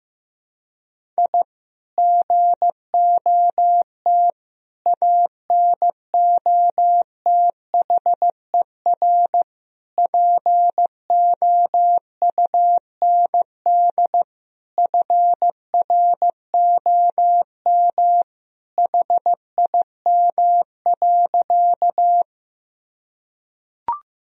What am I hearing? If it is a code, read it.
Morse code, heard as a keyed tone, about 15 wpm, 700 Hz. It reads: I GOT ANOTHER POUND FROM HIM.
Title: Sentences from Top 1000 Words - Review Part 2 - 15wpm